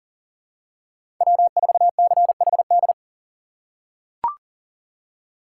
Morse code, heard as a keyed tone, about 39 wpm, 700 Hz. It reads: W4CHD
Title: USA Call Signs - 5 Characters - 2x Character Spacing - 40wpm